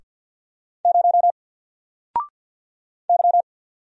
Morse code, heard as a keyed tone, about 50 wpm, 700 Hz. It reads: 0 Y
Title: Single Character (Letter:Number:Punctuation:Prosign) 50wpm